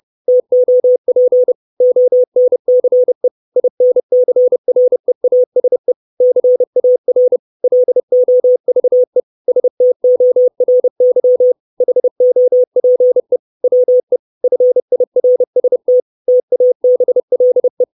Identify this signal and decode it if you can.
Morse code, heard as a keyed tone, about 30 wpm, 500 Hz. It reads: TOP ONCE INCREASE CAR LOVE STORY HOPE WE FIRST TABLE
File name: 632 words / 632 Words 30WPM